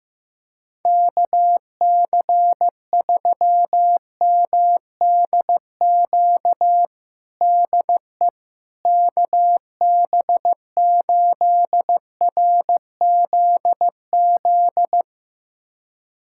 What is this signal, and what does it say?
Morse code, heard as a keyed tone, about 15 wpm, 700 Hz. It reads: KC3MDQ DE KB8RZZ
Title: USA Callsign DE Callsign 15wpm